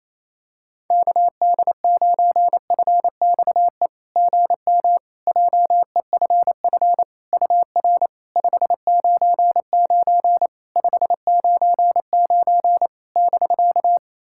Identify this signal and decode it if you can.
Morse code, heard as a keyed tone, about 28 wpm, 700 Hz. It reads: KD9FXE GM JEFF UR 599 599 <BK>
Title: SOTA QSO Practice 28wpm